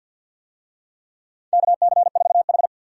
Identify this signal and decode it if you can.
Morse code, heard as a keyed tone, about 52 wpm, 700 Hz. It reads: KC4H
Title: USA Call Signs - 4 Characters - Repeated 2x - 50wpm